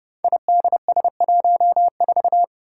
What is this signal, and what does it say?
Morse code, heard as a keyed tone, about 30 wpm, 700 Hz. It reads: IDS14
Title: Five Challenging Letter-Number Combinations 30wpm